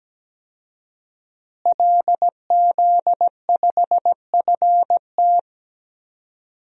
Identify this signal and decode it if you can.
Morse code, heard as a keyed tone, about 17 wpm, 700 Hz. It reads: LZ5FT
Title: DXCC-100 Call Signs - 5 Characters - Repeated 2x - 17wpm